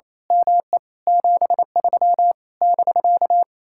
Morse code, heard as a keyed tone, about 26 wpm, 700 Hz. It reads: ME 73 <BK>
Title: Speed-Racing Common QSO Phrases - 28wpm